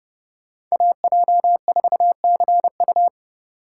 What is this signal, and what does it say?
Morse code, heard as a keyed tone, about 33 wpm, 700 Hz. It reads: AJ4CU